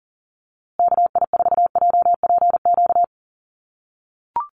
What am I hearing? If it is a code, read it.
Morse code, heard as a keyed tone, about 40 wpm, 700 Hz. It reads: KI4JPQ
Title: USA Call Signs Repeated 3x 40wpm